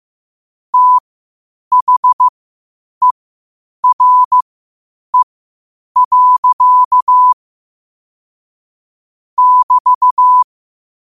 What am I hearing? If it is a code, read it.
Morse code, heard as a keyed tone, about 15 wpm, spaced as for 8 wpm, 1000 Hz. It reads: THERE. <BT>
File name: qotdfiles / 8wpm